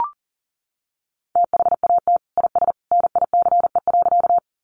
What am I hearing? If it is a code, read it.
Morse code, heard as a keyed tone, about 40 wpm, 700 Hz. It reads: THAT IS NICE.
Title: Sentences from Top 1100 Words 40wpm